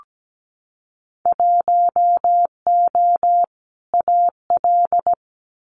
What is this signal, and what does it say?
Morse code, heard as a keyed tone, about 17 wpm, 700 Hz. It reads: 1O AL
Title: Winter Field Day Exchanges 17wpm